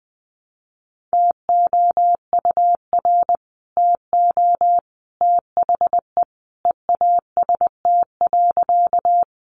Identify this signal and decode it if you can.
Morse code, heard as a keyed tone, about 20 wpm, 700 Hz. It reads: TOUR TO THE EAST.
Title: Sentences from Top 1400 Words 20wpm